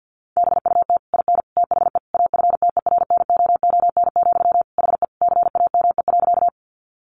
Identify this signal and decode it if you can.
Morse code, heard as a keyed tone, about 50 wpm, 700 Hz. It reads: BUT IN THE AFTERNOON, HE CAME.
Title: Sentences from Top 1500 Words 50wpm